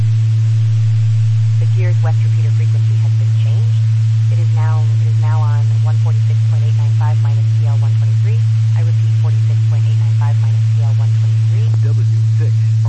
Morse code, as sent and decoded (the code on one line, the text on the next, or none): none